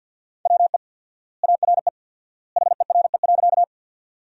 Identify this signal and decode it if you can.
Morse code, heard as a keyed tone, about 50 wpm, 700 Hz. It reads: WE ARE HERE.
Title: 4x Word-Spacing - Sentences from Top 200 Words - 50wpm